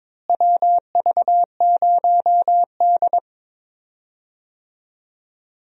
Morse code, heard as a keyed tone, about 22 wpm, 700 Hz. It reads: WV0D
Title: USA Call Signs - 4 Characters - Repeated 3x - 22wpm